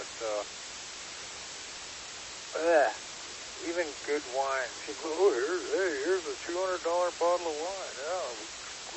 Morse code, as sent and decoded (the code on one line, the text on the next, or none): none